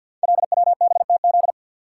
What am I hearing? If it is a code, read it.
Morse code, heard as a keyed tone, about 51 wpm, 700 Hz. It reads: LWBTZ
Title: Code Groups - 5 Letter:Numbers - 50wpm